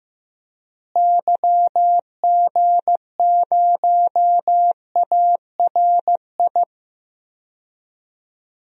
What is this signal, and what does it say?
Morse code, heard as a keyed tone, about 15 wpm, 700 Hz. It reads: YG0ARI
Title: DXCC-100 Call Signs - 6 Characters - Repeated 3x - 15wpm